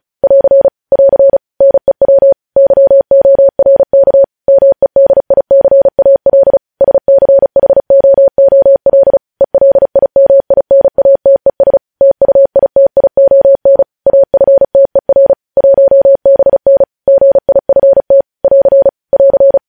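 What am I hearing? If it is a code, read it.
Morse code, heard as a keyed tone, about 36 wpm, 550 Hz. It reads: <AR> <AR> NEW YORK MEDICAL SCHOOL ELIMINATES TUITION AFTER 1BN GIFT <AR> <AR>
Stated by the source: News headlines sent in Morse at 35 WPM.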